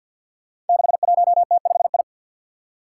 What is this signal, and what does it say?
Morse code, heard as a keyed tone, about 49 wpm, 700 Hz. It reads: 61T5I
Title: Five Challenging Letter-Number Combinations 50wpm